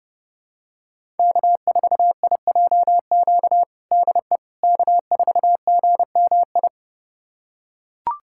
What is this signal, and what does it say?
Morse code, heard as a keyed tone, about 30 wpm, 700 Hz. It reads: K4IJQ DE K4GMI
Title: USA Callsign DE Callsign 30wpm